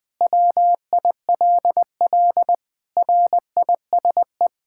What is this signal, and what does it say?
Morse code, heard as a keyed tone, about 21 wpm, 700 Hz. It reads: WILL RISE
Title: Sentences from Top 1000 Words - Review Part 6 - 20wpm